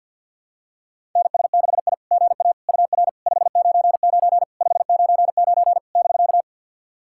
Morse code, heard as a keyed tone, about 52 wpm, 690 Hz. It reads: NS6I GA UR 599 599 <BK>
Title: SOTA QSO Practice 50wpm